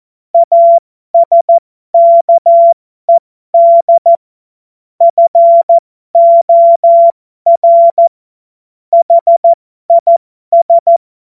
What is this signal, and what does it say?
Morse code, heard as a keyed tone, about 14 wpm, spaced as for 12 wpm, 680 Hz. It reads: ASKED FOR HIS